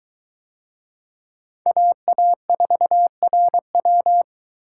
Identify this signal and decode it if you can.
Morse code, heard as a keyed tone, about 21 wpm, 700 Hz. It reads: AA4RW